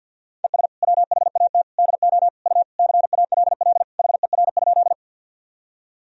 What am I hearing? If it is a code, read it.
Morse code, heard as a keyed tone, about 53 wpm, 700 Hz. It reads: ES WHAT DO U CALL HER?
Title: Sentences from Top 100 Words and Abbreviations 50wpm